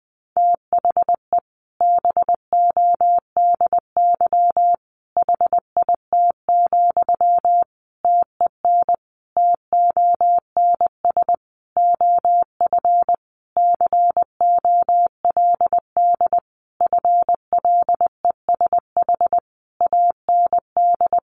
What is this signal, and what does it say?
Morse code, heard as a keyed tone, about 19 wpm, 700 Hz. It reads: THE BODY HIT, TEN TONS OF COLD FLESH AND
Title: A Sound Of Thunder 20wpm